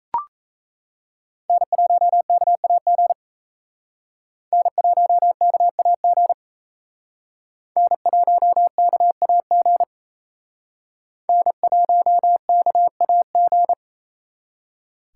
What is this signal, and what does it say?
Morse code, heard as a keyed tone, about 36 wpm, 700 Hz. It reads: N1KAG N1KAG N1KAG N1KAG